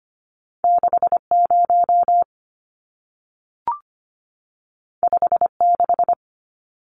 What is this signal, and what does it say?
Morse code, heard as a keyed tone, about 25 wpm, 700 Hz. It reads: 60 56
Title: Numbers - 2-Digit 25wpm